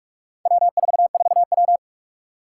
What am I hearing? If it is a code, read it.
Morse code, heard as a keyed tone, about 46 wpm, 700 Hz. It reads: W44W
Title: Four Challenging Letter-Number Combinations 45wpm